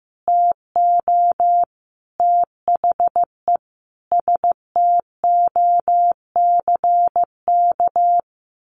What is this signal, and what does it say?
Morse code, heard as a keyed tone, about 15 wpm, 700 Hz. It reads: TO THE STOCK
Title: Sentences from Top 1500 Words 15wpm